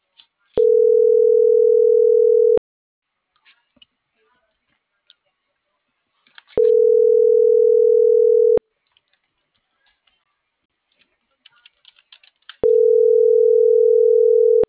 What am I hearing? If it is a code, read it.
Ringback tone (North America), 440 + 480 Hz.